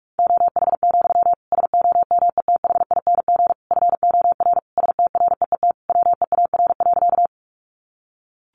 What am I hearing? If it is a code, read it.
Morse code, heard as a keyed tone, about 44 wpm, 700 Hz. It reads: OH, SOMETHING FOR STREET WEAR.
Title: Sentences from Top 1500 Words 45wpm